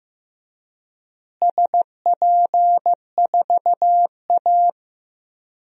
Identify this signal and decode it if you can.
Morse code, heard as a keyed tone, about 15 wpm, 700 Hz. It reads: SP4A
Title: DXCC-100 Call Signs - 4 Characters - Repeated 3x - 15wpm